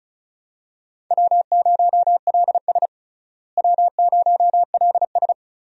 Morse code, heard as a keyed tone, about 34 wpm, 700 Hz. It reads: W0LS W0LS
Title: USA Call Signs - 4 Characters - Repeated 2x - 35wpm